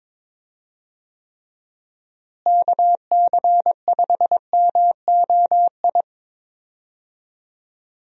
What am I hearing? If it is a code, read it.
Morse code, heard as a keyed tone, about 22 wpm, 700 Hz. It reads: KC5MOI
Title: USA Call Signs - 6 Characters - Repeated 3x - 22wpm